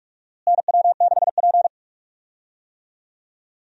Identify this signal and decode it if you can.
Morse code, heard as a keyed tone, about 46 wpm, 700 Hz. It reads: NW6P
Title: USA Call Signs - 4 Characters - Repeated 3x - 45wpm